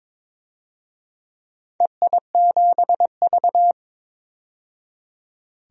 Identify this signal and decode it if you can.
Morse code, heard as a keyed tone, about 21 wpm, 700 Hz. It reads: EI7V
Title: DXCC-100 Call Signs - 4 Characters - 22wpm